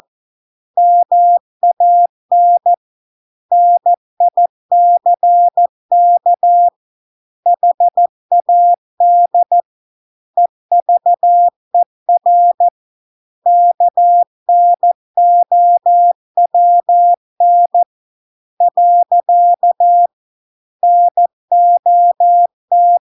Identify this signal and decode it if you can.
Morse code, heard as a keyed tone, about 14 wpm, 700 Hz. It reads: MAN NICK HAD EVER KNOWN . NOT